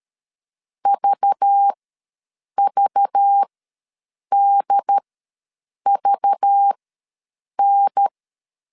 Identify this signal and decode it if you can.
Morse code, heard as a keyed tone, about 13 wpm, spaced as for 7 wpm, 800 Hz. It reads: VVDVN
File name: LEZIONI / 5 LBVC / 5 LBV ws 12wpm 3015